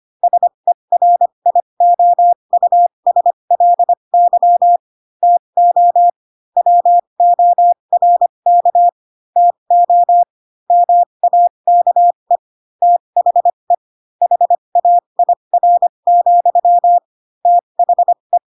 Morse code, heard as a keyed tone, about 25 wpm, spaced as for 22 wpm, 700 Hz. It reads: SERIOUSLY TO WORK TO MAKE THE HAIR, THE
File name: Pinocchio / Pinocchio ch3 18 25 ik1hge